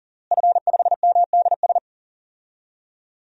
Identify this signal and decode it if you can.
Morse code, heard as a keyed tone, about 40 wpm, 700 Hz. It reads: F5MDS
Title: DXCC-100 Call Signs - 5 Characters - Repeated 3x - 40wpm